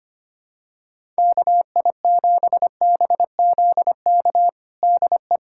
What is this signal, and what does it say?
Morse code, heard as a keyed tone, about 25 wpm, 700 Hz. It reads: KI7BZK DE
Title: USA Callsign DE Callsign - 6 Characters - 25wpm